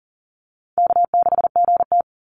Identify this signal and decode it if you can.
Morse code, heard as a keyed tone, about 40 wpm, 700 Hz. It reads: K6GT
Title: Call Signs - 40wpm